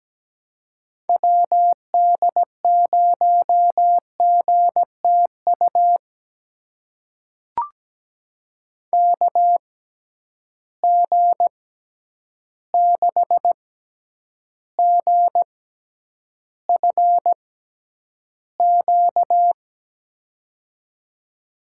Morse code, heard as a keyed tone, about 17 wpm, spaced as for 9 wpm, 700 Hz. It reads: WD0GTU KG6GFQ
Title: USA Call Signs - 6 Characters - 6x Character Spacing - 17wpm